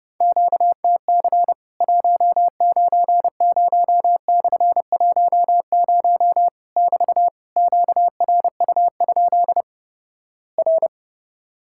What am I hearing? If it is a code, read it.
Morse code, heard as a keyed tone, about 30 wpm, 700 Hz. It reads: QTC 190/10 <BT> QRU? R
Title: Worked All Europe DX Contest - QTC Exchange - 30wpm